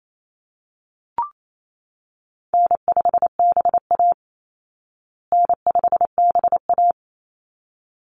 Morse code, heard as a keyed tone, about 28 wpm, 700 Hz. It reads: N5BA N5BA
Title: Call Signs Repeated 3x - 28wpm